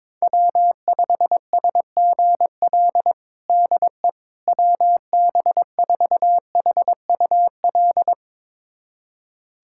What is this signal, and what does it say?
Morse code, heard as a keyed tone, about 22 wpm, 700 Hz. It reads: W5SGL DE WB4HUL
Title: USA Callsign DE Callsign 22wpm